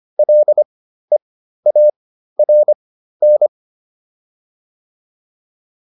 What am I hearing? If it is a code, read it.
Morse code, heard as a keyed tone, about 25 wpm, spaced as for 13 wpm, 600 Hz. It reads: LEARN
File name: PR402 13